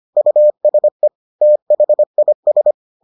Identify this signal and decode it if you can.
Morse code, heard as a keyed tone, about 25 wpm, 600 Hz. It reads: USE THIS